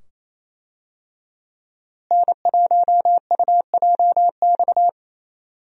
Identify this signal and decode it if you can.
Morse code, heard as a keyed tone, about 28 wpm, 700 Hz. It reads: N1UJX
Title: USA Call Signs - 5 Characters - Repeated 2x - 28wpm